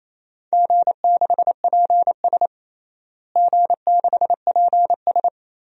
Morse code, heard as a keyed tone, about 28 wpm, 700 Hz. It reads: G6PS G6PS
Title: DXCC-100 Call Signs - 4 Characters - Repeated 2x - 28wpm